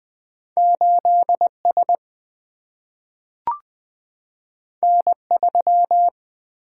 Morse code, heard as a keyed tone, about 20 wpm, 700 Hz. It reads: 8 S N 3
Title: Single-Letter Single-Number 20wpm